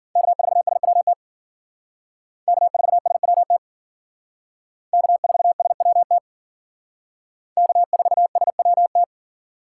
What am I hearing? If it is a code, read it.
Morse code, heard as a keyed tone, about 51 wpm, 700 Hz. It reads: K4SWT K4SWT K4SWT K4SWT